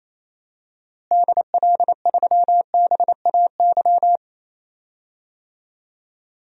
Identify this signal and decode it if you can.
Morse code, heard as a keyed tone, about 27 wpm, 700 Hz. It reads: DL3BAY
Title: DXCC-100 Call Signs - 6 Characters - Repeated 3x - 28wpm